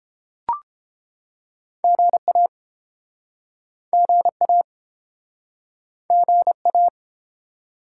Morse code, heard as a keyed tone, about 30 wpm, 700 Hz. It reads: GA GA GA